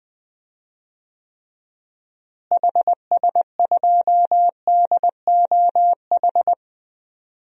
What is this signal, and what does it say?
Morse code, heard as a keyed tone, about 20 wpm, 700 Hz. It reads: HS2DOH
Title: DXCC-100 Call Signs - 6 Characters - Repeated 2x - 20wpm